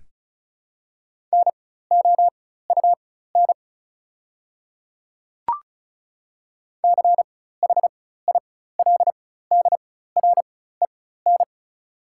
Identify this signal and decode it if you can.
Morse code, heard as a keyed tone, about 36 wpm, spaced as for 16 wpm, 690 Hz. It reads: NOUN CHILDREN
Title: 4x Character Spacing - Top 500 Words - 35wpm